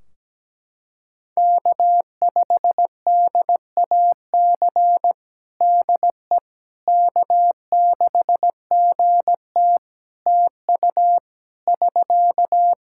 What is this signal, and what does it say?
Morse code, heard as a keyed tone, about 17 wpm, 700 Hz. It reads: K5DAC DE K6GT TU <SK>
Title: Practice QSO - Exchange 4 - Ending - 17wpm